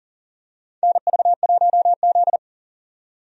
Morse code, heard as a keyed tone, about 40 wpm, 700 Hz. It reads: NV1Z
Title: USA Call Signs - 4 Characters - 40wpm